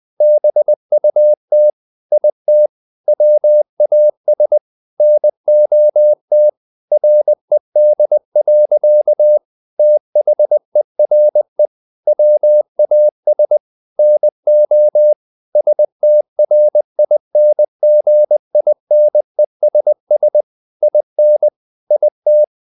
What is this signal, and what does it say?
Morse code, heard as a keyed tone, about 20 wpm, 600 Hz. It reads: BUT IT WAS NOT RED. THERE WAS NO STRINGINESS IN IT
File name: the old man and the sea part 2 20wpm